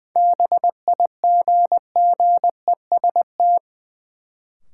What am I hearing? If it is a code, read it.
Morse code, heard as a keyed tone, about 20 wpm, 700 Hz. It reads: BIGGEST
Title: Sets of 4 Words - New Word Drill - 20wpm